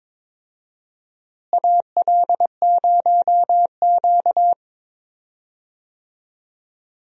Morse code, heard as a keyed tone, about 22 wpm, 700 Hz. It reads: AL0Q
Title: USA Call Signs - 4 Characters - Repeated 3x - 22wpm